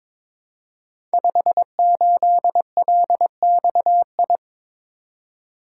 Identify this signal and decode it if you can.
Morse code, heard as a keyed tone, about 23 wpm, 700 Hz. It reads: 58LXI
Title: Code Groups - 5 Letter:Numbers - 22wpm